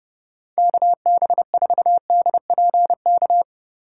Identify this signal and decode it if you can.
Morse code, heard as a keyed tone, about 31 wpm, 700 Hz. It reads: KB4DPK
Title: USA Call Signs - 4x Character Spacing - 30wpm